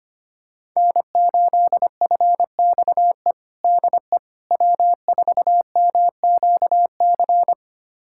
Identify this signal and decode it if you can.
Morse code, heard as a keyed tone, about 25 wpm, 700 Hz. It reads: N8FXE DE W4MQC
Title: USA Callsign DE Callsign 25wpm